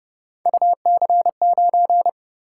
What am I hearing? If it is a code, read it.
Morse code, heard as a keyed tone, about 28 wpm, 700 Hz. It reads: UC9